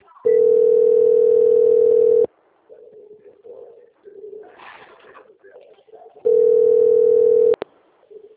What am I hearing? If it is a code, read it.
Ringback tone (North America), 440 + 480 Hz.
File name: out- - - - - .26311